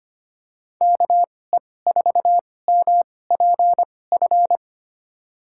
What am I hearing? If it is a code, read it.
Morse code, heard as a keyed tone, about 25 wpm, spaced as for 18 wpm, 700 Hz. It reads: KE4MPF
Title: USA Call Signs - 6 Characters - 2x Character Spacing - 25wpm